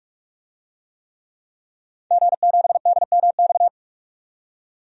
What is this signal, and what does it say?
Morse code, heard as a keyed tone, about 43 wpm, 700 Hz. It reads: G7DMX
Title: DXCC-100 Call Signs - 5 Characters - 45wpm